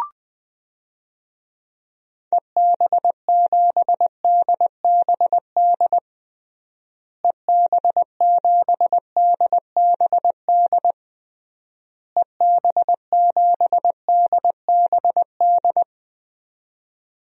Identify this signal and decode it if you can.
Morse code, heard as a keyed tone, about 21 wpm, 700 Hz. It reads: EB7DBD EB7DBD EB7DBD
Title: DXCC-100 Call Signs - 6 Characters - Repeated 3x - 20wpm